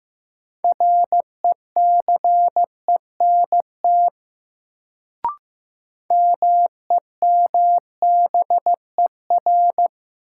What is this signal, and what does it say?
Morse code, heard as a keyed tone, about 15 wpm, 700 Hz. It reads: RECENT MEMBER
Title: Sets of 3 Words - New Word Drill - 15wpm